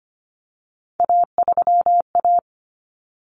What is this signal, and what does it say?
Morse code, heard as a keyed tone, about 25 wpm, 700 Hz. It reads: A3A
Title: Common Cut Numbers - 3-Digit 25wpm